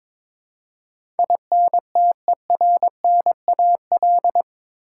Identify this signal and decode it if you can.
Morse code, heard as a keyed tone, about 22 wpm, 700 Hz. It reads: INTERNAL
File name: practice / 22wpm / Eight Letter Words - 22wpm